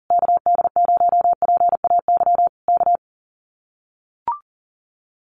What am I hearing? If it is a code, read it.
Morse code, heard as a keyed tone, about 40 wpm, 700 Hz. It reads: KD0PAY K
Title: Common QSO Phrases Encore 40wpm